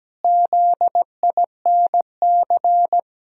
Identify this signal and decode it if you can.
Morse code, heard as a keyed tone, about 17 wpm, 700 Hz. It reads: ZINC
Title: Pangram Challenge v2 - 17wpm